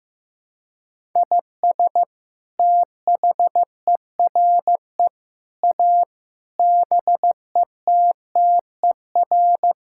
Morse code, heard as a keyed tone, about 15 wpm, 700 Hz. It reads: IS THERE A BETTER
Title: Sentences from Top 1000 Words - Review Part 6 - 15wpm